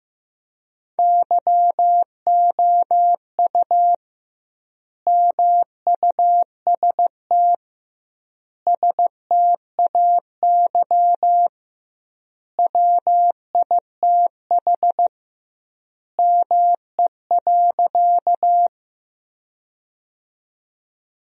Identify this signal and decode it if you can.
Morse code, heard as a keyed tone, about 15 wpm, 700 Hz. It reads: YOU MUST STAY WITH ME.
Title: 2x Word-Spacing - Sentences from Top 500 Words - 15wpm